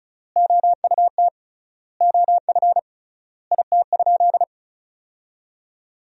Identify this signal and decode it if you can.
Morse code, heard as a keyed tone, about 34 wpm, 700 Hz. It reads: OUT OF IT?
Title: 3x Word-Spacing - Sentences from Top 300 Words - 35wpm